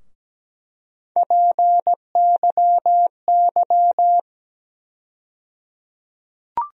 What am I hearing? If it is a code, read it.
Morse code, heard as a keyed tone, about 17 wpm, 700 Hz. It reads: PYY
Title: USA Call Sign Postfixes - 3 Letters 17wpm